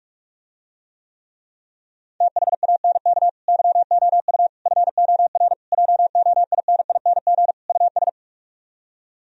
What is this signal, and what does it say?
Morse code, heard as a keyed tone, about 49 wpm, 700 Hz. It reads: THANK YOU FOR JOINING US